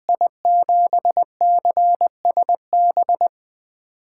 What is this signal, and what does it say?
Morse code, heard as a keyed tone, about 20 wpm, 700 Hz. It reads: I7CSB
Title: USA Call Signs - 8x Character Spacing - 20wpm